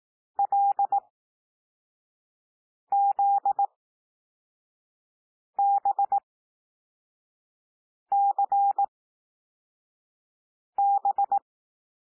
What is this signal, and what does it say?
Morse code, heard as a keyed tone, about 18 wpm, spaced as for 4 wpm, 800 Hz. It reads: LZBCB
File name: cwmp3 / L2 SOUND LETTERS 18 wpm